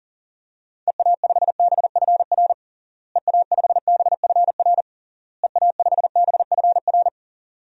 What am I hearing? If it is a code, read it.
Morse code, heard as a keyed tone, about 37 wpm, 700 Hz. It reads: EA5BFR EA5BFR EA5BFR
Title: DXCC-100 Call Signs - 6 Characters - Repeated 3x - 40wpm